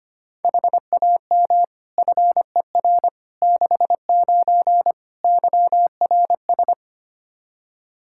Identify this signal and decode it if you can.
Morse code, heard as a keyed tone, about 25 wpm, 700 Hz. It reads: HAM FER 69 YRS
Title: K1USN SST QSO Practice 25wpm